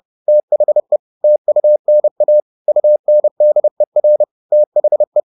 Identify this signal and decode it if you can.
Morse code, heard as a keyed tone, about 30 wpm, 600 Hz. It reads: THE TUNA UNDER THE
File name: the old man and the sea / the old man and the sea 1 30wpm